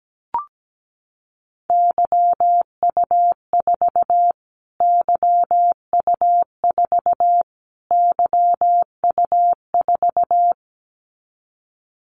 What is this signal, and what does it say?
Morse code, heard as a keyed tone, about 17 wpm, 700 Hz. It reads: YU4 YU4 YU4
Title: Callsign Prefixes for Top 25 DXCC - Repeated 3x - 17wpm